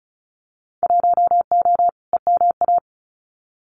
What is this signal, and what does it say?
Morse code, heard as a keyed tone, about 36 wpm, 700 Hz. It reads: 1O EMA
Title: Winter Field Day Exchanges Encore 35wpm